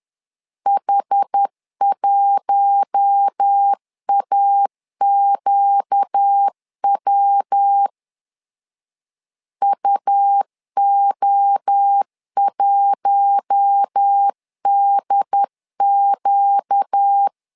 Morse code, heard as a keyed tone, about 10 wpm, 800 Hz. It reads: H1AQW UO1DQ
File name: LEZIONI / 4 Q1 10wpm 2448